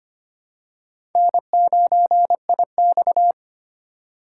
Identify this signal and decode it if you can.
Morse code, heard as a keyed tone, about 25 wpm, 700 Hz. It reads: N9IX
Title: USA Call Signs - 4 Characters - 2x Character Spacing - 25wpm